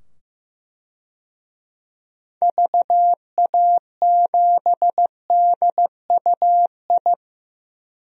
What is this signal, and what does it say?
Morse code, heard as a keyed tone, about 15 wpm, 700 Hz. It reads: VA7DUI
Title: DXCC-100 Call Signs - 6 Characters - 15wpm